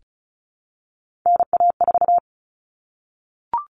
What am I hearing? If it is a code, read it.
Morse code, heard as a keyed tone, about 34 wpm, 700 Hz. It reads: NA4
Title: All Cut Numbers - 3-Digit 35wpm